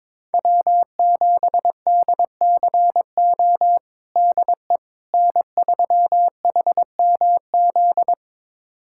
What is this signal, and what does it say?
Morse code, heard as a keyed tone, about 22 wpm, 700 Hz. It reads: W7DCO DE N3HMZ
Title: USA Callsign DE Callsign - 5 Characters - 22wpm